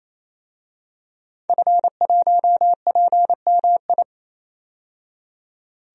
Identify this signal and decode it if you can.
Morse code, heard as a keyed tone, about 28 wpm, 700 Hz. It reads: F1PMI
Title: DXCC-100 Call Signs - 5 Characters - 28wpm